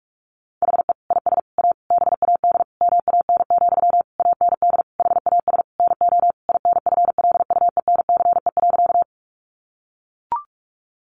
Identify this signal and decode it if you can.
Morse code, heard as a keyed tone, about 44 wpm, 700 Hz. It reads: HE IS A BAD MAN, AND HAS NO INFLUENCE.
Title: Sentences from Top 1100 Words 45wpm